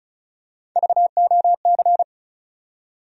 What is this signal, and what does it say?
Morse code, heard as a keyed tone, about 36 wpm, 700 Hz. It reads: VOC
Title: USA Call Sign Postfixes - 3 Letters 35wpm